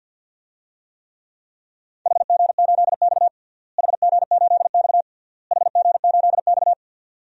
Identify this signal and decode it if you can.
Morse code, heard as a keyed tone, about 49 wpm, 700 Hz. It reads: HG8X HG8X HG8X
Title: DXCC-100 Call Signs - 4 Characters - Repeated 3x - 50wpm